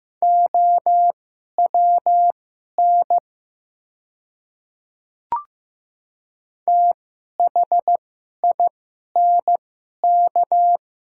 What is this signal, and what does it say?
Morse code, heard as a keyed tone, about 15 wpm, spaced as for 11 wpm, 700 Hz. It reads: OWN THINK
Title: 2x Character Spacing - Top 300 Words - 15wpm